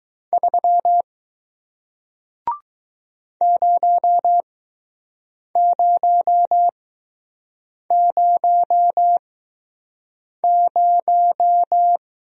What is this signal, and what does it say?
Morse code, heard as a keyed tone, about 20 wpm, spaced as for 6 wpm, 700 Hz. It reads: 3 0000